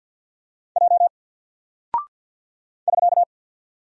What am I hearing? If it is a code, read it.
Morse code, heard as a keyed tone, about 48 wpm, 700 Hz. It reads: J <SK>
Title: Single Character (Letter:Number:Punctuation:Prosign) Alt 50wpm